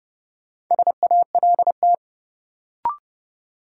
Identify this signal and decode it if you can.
Morse code, heard as a keyed tone, about 30 wpm, 700 Hz. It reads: SALT
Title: Sentences from Top 800 Words - New Word Drill - 30wpm